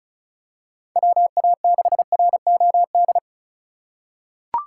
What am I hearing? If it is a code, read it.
Morse code, heard as a keyed tone, about 35 wpm, 700 Hz. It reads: WA6ROD
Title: USA Call Signs - 6 Characters - 3x Character Spacing - 35wpm